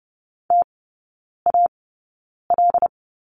Morse code, heard as a keyed tone, about 30 wpm, spaced as for 9 wpm, 700 Hz. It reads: TAL
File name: Lesson 18e - Letter m - Three Characters 30wpm